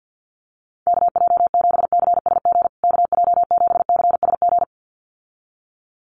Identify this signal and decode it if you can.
Morse code, heard as a keyed tone, about 50 wpm, 700 Hz. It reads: KJ7CSG KJ7CSG
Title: USA Call Signs Repeated 2x 50wpm